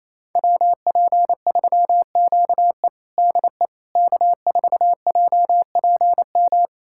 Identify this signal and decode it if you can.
Morse code, heard as a keyed tone, about 28 wpm, 700 Hz. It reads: WP3QE DE K4JPM
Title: USA Callsign DE Callsign - 5 Characters - 28wpm